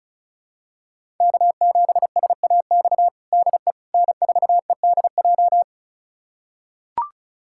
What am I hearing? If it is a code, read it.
Morse code, heard as a keyed tone, about 36 wpm, 700 Hz. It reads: K7SAX DE N4EDJ
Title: USA Callsign DE Callsign - 5 Characters - 35wpm